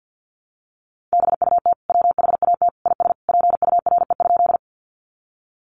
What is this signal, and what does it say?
Morse code, heard as a keyed tone, about 50 wpm, 700 Hz. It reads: BUT WHAT IS PURE?
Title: Sentences from Top 1500 Words 50wpm